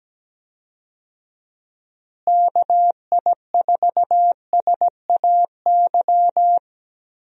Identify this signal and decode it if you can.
Morse code, heard as a keyed tone, about 17 wpm, 700 Hz. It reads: KI4SAY
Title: USA Call Signs - 6 Characters - Repeated 2x - 17wpm